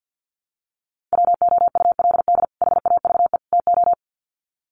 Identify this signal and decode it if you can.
Morse code, heard as a keyed tone, about 50 wpm, 700 Hz. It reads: WOULD HAVE TO
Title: Sets of 3 Words 50wpm